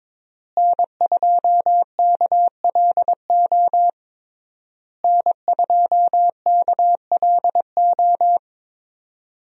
Morse code, heard as a keyed tone, about 22 wpm, 700 Hz. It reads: N2KLO N2KLO
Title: USA Call Signs - 5 Characters - Repeated 3x - 22wpm